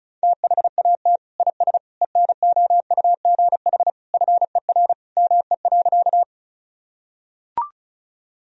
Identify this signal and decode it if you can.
Morse code, heard as a keyed tone, about 34 wpm, 700 Hz. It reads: THAT IS ENOUGH FER ME.
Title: Sentences from Top 400 Words and Abbreviations 35wpm